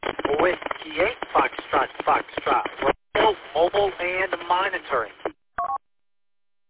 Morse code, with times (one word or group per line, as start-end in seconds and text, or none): none